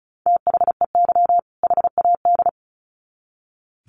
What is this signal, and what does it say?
Morse code, heard as a keyed tone, about 34 wpm, 700 Hz. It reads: THEY HAD
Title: Sets of 2 Words 35wpm